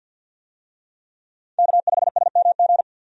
Morse code, heard as a keyed tone, about 50 wpm, 700 Hz. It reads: K5SMG
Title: USA Call Signs - 5 Characters - 50wpm